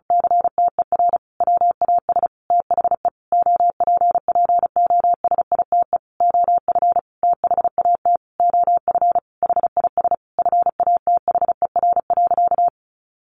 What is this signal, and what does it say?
Morse code, heard as a keyed tone, about 34 wpm, 700 Hz. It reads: CTER WAS THE OPPOSITE OF THAT OF HIS FATHER.
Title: Sentences from Top 1300 Words 35wpm